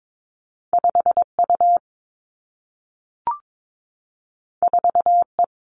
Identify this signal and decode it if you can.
Morse code, heard as a keyed tone, about 22 wpm, 700 Hz. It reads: 5 U 4 E
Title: Two Challenging Letter-Number Combinations 22wpm